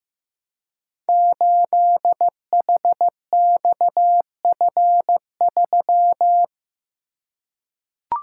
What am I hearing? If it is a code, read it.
Morse code, heard as a keyed tone, about 15 wpm, 700 Hz. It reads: 8HXF3
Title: Code Groups - 5 Letter:Numbers - 15wpm